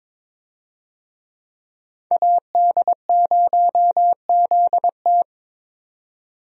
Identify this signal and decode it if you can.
Morse code, heard as a keyed tone, about 22 wpm, 700 Hz. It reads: AD0ZT
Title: USA Call Signs - 5 Characters - 22wpm